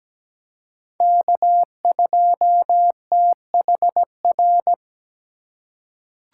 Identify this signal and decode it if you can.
Morse code, heard as a keyed tone, about 17 wpm, 700 Hz. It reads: K2THR
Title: USA Call Signs - 2x Character Spacing - 17wpm